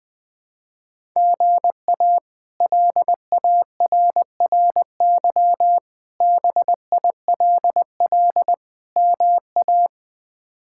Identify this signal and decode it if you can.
Morse code, heard as a keyed tone, about 21 wpm, 700 Hz. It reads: GA LARRY BILL MA
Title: K1USN SST QSO Practice - 20wpm